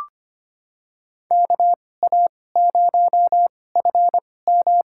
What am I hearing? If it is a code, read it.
Morse code, heard as a keyed tone, about 25 wpm, spaced as for 18 wpm, 700 Hz. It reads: KA0FM
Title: USA Call Signs - 2x Character Spacing - 25wpm